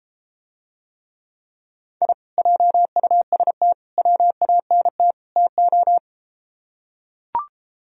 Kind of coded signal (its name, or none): Morse code